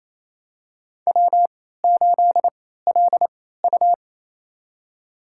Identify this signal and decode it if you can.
Morse code, heard as a keyed tone, about 28 wpm, spaced as for 16 wpm, 700 Hz. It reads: W8LU
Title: USA Call Signs - 4 Characters - 3x Character Spacing - 28wpm